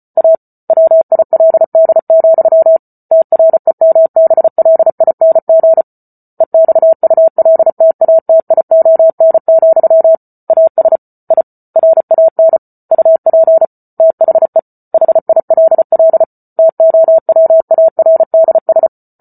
Morse code, heard as a keyed tone, about 34 wpm, 650 Hz. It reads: A WILD, TREMBLING EXULTATION, AS I RAN UP THE HILL TOWARDS
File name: Book Two / 08-Chapter 8 - 34 WPM